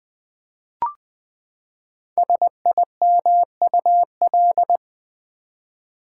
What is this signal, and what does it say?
Morse code, heard as a keyed tone, about 20 wpm, 700 Hz. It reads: SIMUL
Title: Build A Word - 5 Letters At A Time - 10-Letter Words - 20wpm